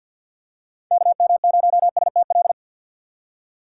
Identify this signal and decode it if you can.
Morse code, heard as a keyed tone, about 51 wpm, 700 Hz. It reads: KM0STL
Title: USA Call Signs - 6 Characters - 3x Character Spacing - 50wpm